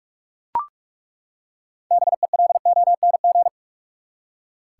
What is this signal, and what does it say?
Morse code, heard as a keyed tone, about 41 wpm, 700 Hz. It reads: BELONG
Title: Six Letter Words - 45wpm